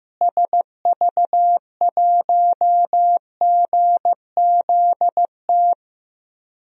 Morse code, heard as a keyed tone, about 15 wpm, 700 Hz. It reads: SV1GZT
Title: DXCC-100 Call Signs - 6 Characters - Repeated 3x - 15wpm